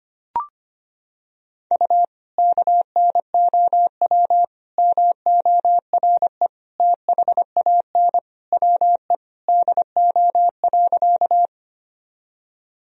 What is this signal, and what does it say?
Morse code, heard as a keyed tone, about 25 wpm, 700 Hz. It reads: U KNOW MORE THAN WE DO.
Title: Sentences from Top 100 Words and Abbreviations 25wpm